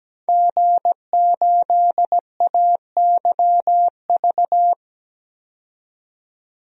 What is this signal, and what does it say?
Morse code, heard as a keyed tone, about 17 wpm, 700 Hz. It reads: G8AYV
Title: DXCC-100 Call Signs - 5 Characters - 17wpm